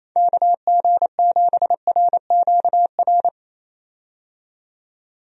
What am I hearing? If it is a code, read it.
Morse code, heard as a keyed tone, about 27 wpm, 700 Hz. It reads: KG7RQR
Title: USA Call Signs - 6 Characters - 28wpm